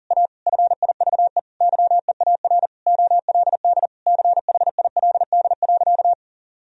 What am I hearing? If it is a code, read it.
Morse code, heard as a keyed tone, about 40 wpm, 700 Hz. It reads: A FIVE YEAR OLD CHILD.
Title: Sentences from Top 1000 Words - Review Part 2 - 40wpm